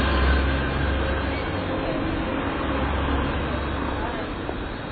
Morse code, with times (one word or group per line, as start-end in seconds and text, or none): none